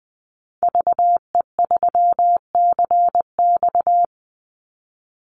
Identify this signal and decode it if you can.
Morse code, heard as a keyed tone, about 20 wpm, 700 Hz. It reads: VE3CX
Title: Call Signs - 20wpm